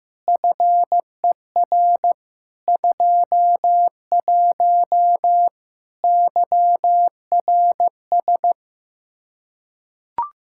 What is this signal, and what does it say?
Morse code, heard as a keyed tone, about 16 wpm, 700 Hz. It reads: FER 21 YRS
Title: Practice QSO - Exchange 3 - More Info - 15wpm